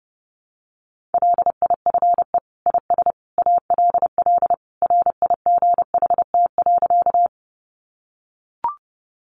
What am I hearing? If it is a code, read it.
Morse code, heard as a keyed tone, about 30 wpm, 700 Hz. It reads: LIFE IS ALL RIGHT.
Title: Sentences from Top 300 Words 30wpm